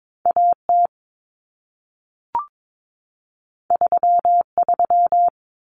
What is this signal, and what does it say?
Morse code, heard as a keyed tone, about 22 wpm, 700 Hz. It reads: AT 33
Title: Common Cut Numbers - 2-Digit 22wpm